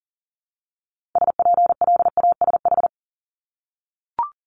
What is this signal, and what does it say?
Morse code, heard as a keyed tone, about 40 wpm, 700 Hz. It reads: SPLASH
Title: Multi-letter Phonemes Repeated 3x 40wpm